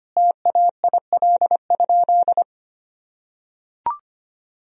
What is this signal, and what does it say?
Morse code, heard as a keyed tone, about 25 wpm, 700 Hz. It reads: TAIL?
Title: Sentences from Top 1000 Words - Review Part 3 - 25wpm